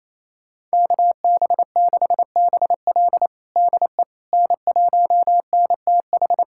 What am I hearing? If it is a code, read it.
Morse code, heard as a keyed tone, about 27 wpm, 700 Hz. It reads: KB6BL DE N1NTH
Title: USA Callsign DE Callsign - 5 Characters - 28wpm